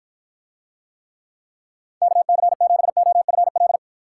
Morse code, heard as a keyed tone, about 53 wpm, 700 Hz. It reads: KC7OFL